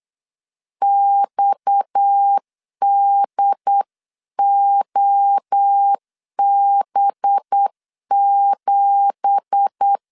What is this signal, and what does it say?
Morse code, heard as a keyed tone, about 8 wpm, 800 Hz. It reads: XDOB7
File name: 8wpm